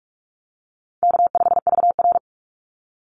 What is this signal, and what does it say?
Morse code, heard as a keyed tone, about 44 wpm, 700 Hz. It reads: K5VR
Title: Call Signs Repeated 3x - 45wpm